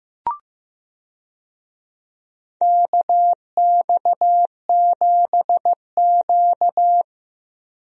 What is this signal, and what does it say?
Morse code, heard as a keyed tone, about 15 wpm, 700 Hz. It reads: KX7Q
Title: USA Call Signs - 4 Characters - Repeated 3x - 15wpm